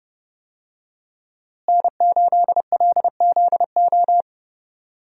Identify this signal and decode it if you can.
Morse code, heard as a keyed tone, about 30 wpm, 700 Hz. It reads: N8LZO
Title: USA Call Signs - 5 Characters - Repeated 3x - 30wpm